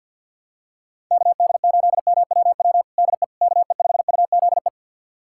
Morse code, heard as a keyed tone, about 50 wpm, 700 Hz. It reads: KD8MWW DE KE5UZE
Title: USA Callsign DE Callsign - 6 Characters - 50wpm